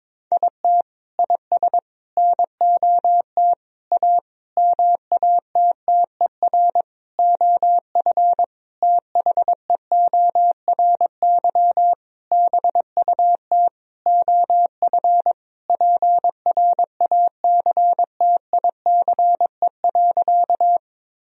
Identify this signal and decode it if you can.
Morse code, heard as a keyed tone, about 22 wpm, 700 Hz. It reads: IT IS NOT A MATTER OF THEORY BUT OF PRACTICE.
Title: Sentences from Top 1000 Words - Review Part 4 - 22wpm